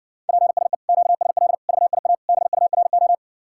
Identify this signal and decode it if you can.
Morse code, heard as a keyed tone, about 60 wpm, 700 Hz. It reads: PSE QSL VIA BURO